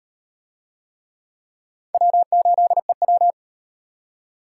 Morse code, heard as a keyed tone, about 36 wpm, 700 Hz. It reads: W8EW